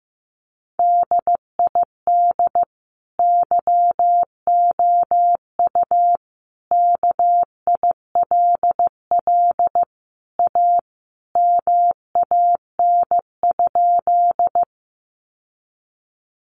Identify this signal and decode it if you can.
Morse code, heard as a keyed tone, about 15 wpm, 700 Hz. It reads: DID YOU KILL A MAN?
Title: Sentences from Top 1400 Words 15wpm